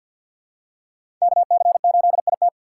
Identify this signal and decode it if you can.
Morse code, heard as a keyed tone, about 47 wpm, 700 Hz. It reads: KC8IT
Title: USA Call Signs - 5 Characters - 4x Character Spacing - 50wpm